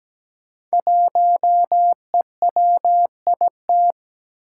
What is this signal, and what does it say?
Morse code, heard as a keyed tone, about 17 wpm, 700 Hz. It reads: 1EWIT
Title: Five Challenging Letter-Number Combinations 17wpm